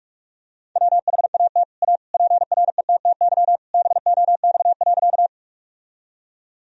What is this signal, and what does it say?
Morse code, heard as a keyed tone, about 46 wpm, 700 Hz. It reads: WHAT A PRETTY BOX.
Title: Sentences from Top 1000 Words - Review Part 2 - 45wpm